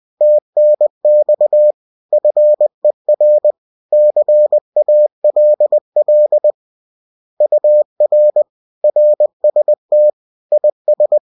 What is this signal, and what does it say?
Morse code, heard as a keyed tone, about 20 wpm, 600 Hz. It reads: TNX FER CALL UR RST IS